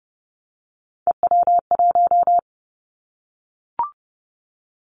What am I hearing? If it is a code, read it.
Morse code, heard as a keyed tone, about 30 wpm, 700 Hz. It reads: EW1
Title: Callsign Prefixes for Top 50 DXCC - 30wpm